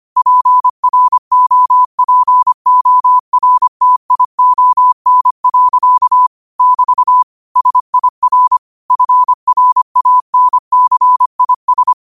Morse code, heard as a keyed tone, about 25 wpm, 1000 Hz. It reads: PROPORTION. <BT> SIR FRANCIS